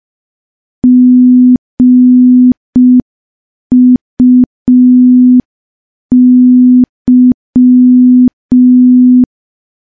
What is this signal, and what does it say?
Morse code, heard as a keyed tone, about 5 wpm, 250 Hz. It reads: GUY
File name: morse / morse-training-05WPM